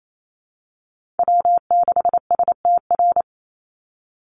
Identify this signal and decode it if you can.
Morse code, heard as a keyed tone, about 28 wpm, 700 Hz. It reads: W6STR
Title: NA SOTA Chaser Callsigns 28wpm